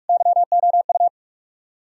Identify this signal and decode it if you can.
Morse code, heard as a keyed tone, about 58 wpm, 700 Hz. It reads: YOU